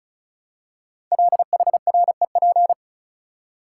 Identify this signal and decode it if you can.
Morse code, heard as a keyed tone, about 36 wpm, 700 Hz. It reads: LHREP
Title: Code Groups - 5 Letters - 35wpm